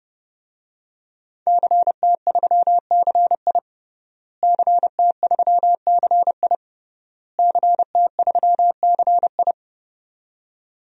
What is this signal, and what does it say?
Morse code, heard as a keyed tone, about 32 wpm, 700 Hz. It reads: CT3CI CT3CI CT3CI
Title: DXCC-100 Call Signs - 5 Characters - Repeated 3x - 30wpm